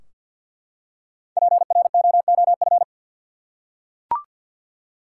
Morse code, heard as a keyed tone, about 52 wpm, 700 Hz. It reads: PROOF
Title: Five Letter Words - 50wpm